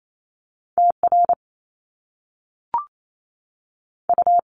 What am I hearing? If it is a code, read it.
Morse code, heard as a keyed tone, about 28 wpm, spaced as for 3 wpm, 700 Hz. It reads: T RU
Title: Top Two Letter Combinations in Words 28wpm